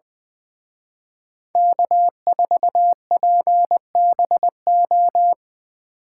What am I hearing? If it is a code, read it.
Morse code, heard as a keyed tone, about 23 wpm, 700 Hz. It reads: K4PBO
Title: Speed Racing - USA 1x3 Call Signs 20wpm